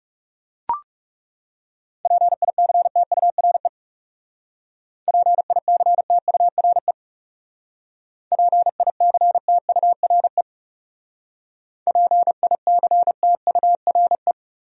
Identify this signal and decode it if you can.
Morse code, heard as a keyed tone, about 35 wpm, 700 Hz. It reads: PICTURE PICTURE PICTURE PICTURE
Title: Speed-Racing Top 200 Words 30wpm